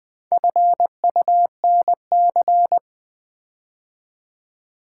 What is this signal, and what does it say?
Morse code, heard as a keyed tone, about 20 wpm, 700 Hz. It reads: FUNC
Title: Build A Word - 4 Letters At A Time - 8-Letter Words - 20wpm